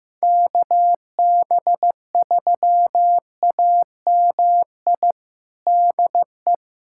Morse code, heard as a keyed tone, about 15 wpm, 700 Hz. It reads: KB3AMI DE
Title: USA Callsign DE Callsign - 6 Characters - 15wpm